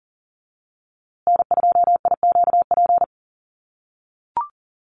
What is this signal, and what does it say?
Morse code, heard as a keyed tone, about 40 wpm, 700 Hz. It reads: N2IQP
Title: USA Call Signs Repeated 3x 40wpm